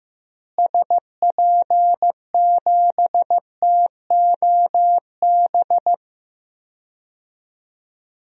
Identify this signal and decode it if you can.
Morse code, heard as a keyed tone, about 16 wpm, 700 Hz. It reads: SP7TOB
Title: DXCC-100 Call Signs - 6 Characters - Repeated 3x - 15wpm